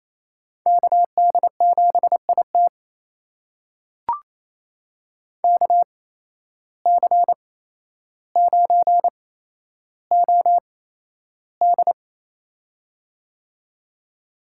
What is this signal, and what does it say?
Morse code, heard as a keyed tone, about 28 wpm, spaced as for 12 wpm, 700 Hz. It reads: KD7IT KC9OD
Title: USA Call Signs - 5 Characters - 8x Character Spacing - 28wpm